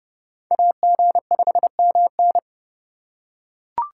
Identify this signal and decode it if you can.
Morse code, heard as a keyed tone, about 29 wpm, 700 Hz. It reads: AG5MN